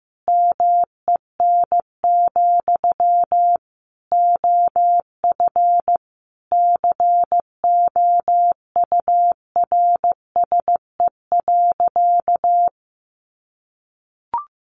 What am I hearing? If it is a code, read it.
Morse code, heard as a keyed tone, about 15 wpm, 700 Hz. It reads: MEN, OF COURSE.
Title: Sentences from Top 1100 Words 15wpm